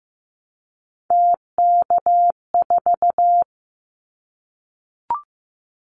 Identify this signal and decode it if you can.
Morse code, heard as a keyed tone, about 15 wpm, 700 Hz. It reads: TK4
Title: Callsign Prefixes for Top 100 DXCC - 15wpm